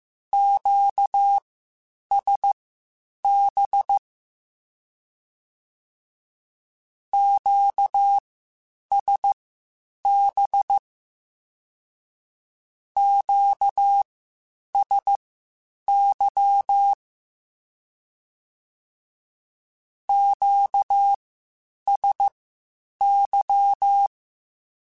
Morse code, heard as a keyed tone, about 15 wpm, spaced as for 8 wpm, 800 Hz. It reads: QSB QSB QSY QSY
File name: (spaced) / Q-Calls Spaced-mp3s / Q Calls Twice sp @13